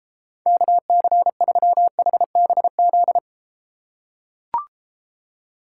Morse code, heard as a keyed tone, about 33 wpm, 700 Hz. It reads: KC3HBZ